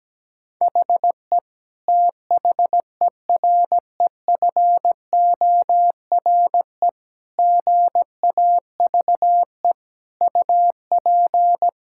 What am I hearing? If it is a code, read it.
Morse code, heard as a keyed tone, about 17 wpm, 700 Hz. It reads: HE THEREFORE GAVE UP
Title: Sentences from Top 1000 Words - Review Part 5 - 17wpm